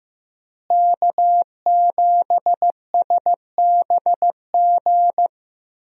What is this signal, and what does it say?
Morse code, heard as a keyed tone, about 16 wpm, 700 Hz. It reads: K7SBG
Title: USA Call Signs - 5 Characters - Repeated 2x - 15wpm